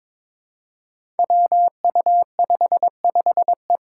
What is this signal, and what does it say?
Morse code, heard as a keyed tone, about 22 wpm, 700 Hz. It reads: WU55E
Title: Five Challenging Letter-Number Combinations 22wpm